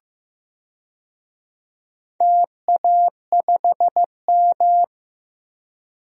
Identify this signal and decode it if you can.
Morse code, heard as a keyed tone, about 15 wpm, 700 Hz. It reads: TA5M
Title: DXCC-100 Call Signs - 4 Characters - Repeated 3x - 15wpm